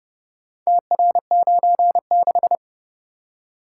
Morse code, heard as a keyed tone, about 31 wpm, 700 Hz. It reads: TR96